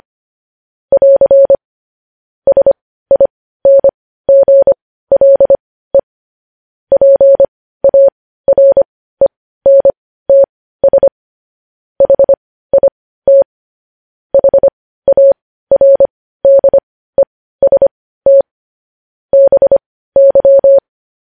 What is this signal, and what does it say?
Morse code, heard as a keyed tone, about 25 wpm, spaced as for 15 wpm, 550 Hz. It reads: <AR> SINGLE PARENTS HIT HARDEST BY
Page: News Headlines in Morse Code at 15 WPM